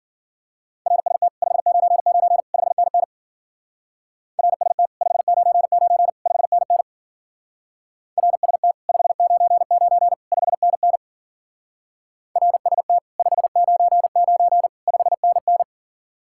Morse code, heard as a keyed tone, about 52 wpm, 700 Hz. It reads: RST 599 5NN RST 599 5NN RST 599 5NN RST 599 5NN